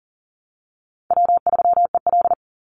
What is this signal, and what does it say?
Morse code, heard as a keyed tone, about 40 wpm, 700 Hz. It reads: W3EL
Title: Easy Call Signs Repeated 3x - 40wpm